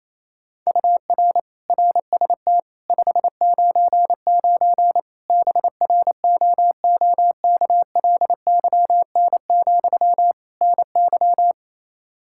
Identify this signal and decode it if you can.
Morse code, heard as a keyed tone, about 28 wpm, 700 Hz. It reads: UR RST 599 BROOKLYN, NY
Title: Practice QSO - Exchange 1 - Beginning - 28wpm